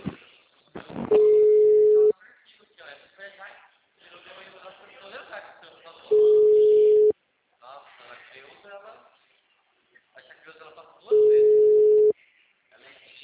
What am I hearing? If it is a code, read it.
Ringback tone (Europe, 425 Hz), with speech.
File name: out- -2011- - - .10